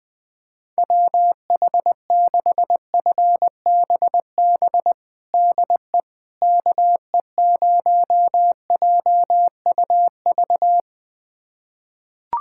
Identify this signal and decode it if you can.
Morse code, heard as a keyed tone, about 20 wpm, 700 Hz. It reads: WH6FBB DE KE0JUV
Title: USA Callsign DE Callsign 20wpm